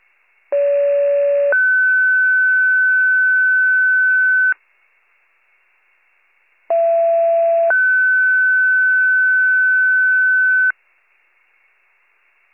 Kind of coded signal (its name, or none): two-tone page